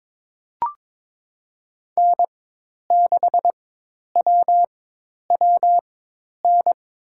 Morse code, heard as a keyed tone, about 22 wpm, spaced as for 10 wpm, 700 Hz. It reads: N6WWN
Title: USA Call Signs - 5 Characters - 4x Character Spacing - 22wpm